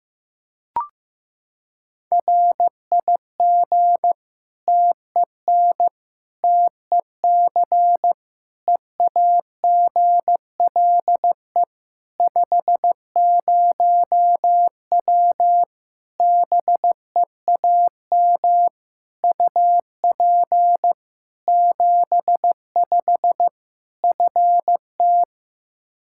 Morse code, heard as a keyed tone, about 16 wpm, 700 Hz. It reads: RIG TEN TEC EAGLE 50W BEAM UP 75 FT
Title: Practice QSO - Exchange 2 - Info - 15wpm